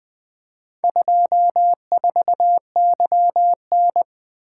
Morse code, heard as a keyed tone, about 20 wpm, 700 Hz. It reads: 24YN
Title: Four Letter-Number Combinations - 20wpm